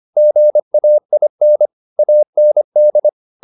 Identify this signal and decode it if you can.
Morse code, heard as a keyed tone, about 25 wpm, 600 Hz. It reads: GAIN AND